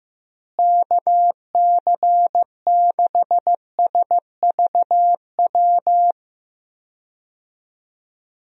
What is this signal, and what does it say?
Morse code, heard as a keyed tone, about 16 wpm, 700 Hz. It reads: KC6SVW
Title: USA Call Signs - 6 Characters - Repeated 2x - 15wpm